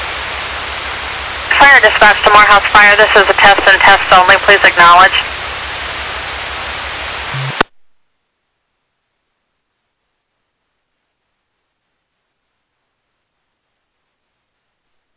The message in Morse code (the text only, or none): none